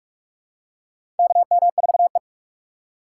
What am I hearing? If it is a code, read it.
Morse code, heard as a keyed tone, about 44 wpm, 700 Hz. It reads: KM4E
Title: USA Call Signs - 3x Character Spacing - 45wpm